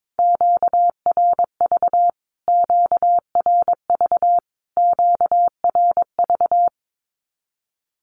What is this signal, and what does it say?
Morse code, heard as a keyed tone, about 22 wpm, 700 Hz. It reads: QRV QRV QRV
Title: Q-Codes Repeated 3x 22wpm